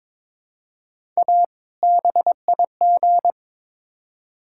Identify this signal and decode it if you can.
Morse code, heard as a keyed tone, about 22 wpm, 700 Hz. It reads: A BIG
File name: Build A Phrase - 2-Word Phrases - 22wpm